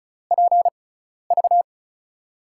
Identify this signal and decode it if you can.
Morse code, heard as a keyed tone, about 36 wpm, spaced as for 12 wpm, 700 Hz. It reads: PV
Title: USA Call Signs - 6x Character Spacing - 35wpm